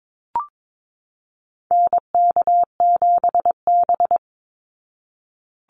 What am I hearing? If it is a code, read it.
Morse code, heard as a keyed tone, about 22 wpm, 700 Hz. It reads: NK7B
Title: Call Signs Repeated 3x - 22wpm